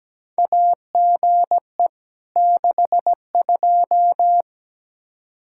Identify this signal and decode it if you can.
Morse code, heard as a keyed tone, about 17 wpm, 700 Hz. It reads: AGE 62
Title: Practice QSO - Exchange 3 - More Info - 17wpm